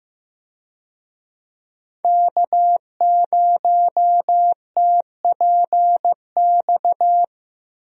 Morse code, heard as a keyed tone, about 15 wpm, 700 Hz. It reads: K0TPX
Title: USA Call Signs - 5 Characters - Repeated 2x - 15wpm